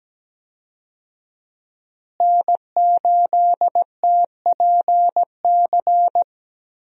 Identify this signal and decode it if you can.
Morse code, heard as a keyed tone, about 17 wpm, 700 Hz. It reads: N8TPC
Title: USA Call Signs - 5 Characters - Repeated 2x - 17wpm